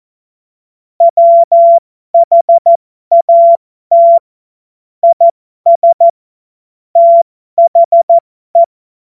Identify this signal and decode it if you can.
Morse code, heard as a keyed tone, about 14 wpm, spaced as for 12 wpm, 680 Hz. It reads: WHAT IS THE
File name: Morsekurs / Witze